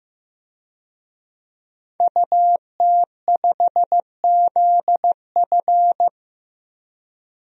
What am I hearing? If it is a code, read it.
Morse code, heard as a keyed tone, about 16 wpm, 700 Hz. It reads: UT5ZF
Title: DXCC-100 Call Signs - 5 Characters - Repeated 3x - 15wpm